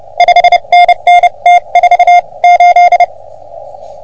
Morse code, heard as a keyed tone, about 28 wpm, spaced as for 25 wpm, 710 Hz. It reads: 5NN T48
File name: OM3PA PILUP CUC 388